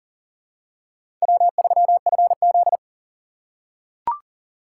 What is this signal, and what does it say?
Morse code, heard as a keyed tone, about 37 wpm, 700 Hz. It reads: W3FZ
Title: USA Call Signs - 6x Character Spacing - 40wpm